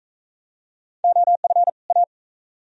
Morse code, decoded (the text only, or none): OF A